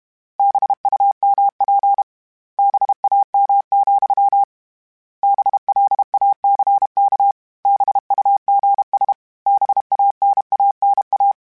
Morse code, heard as a keyed tone, about 32 wpm, 800 Hz. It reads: BUMP BAM, BLACK BUGS BANANA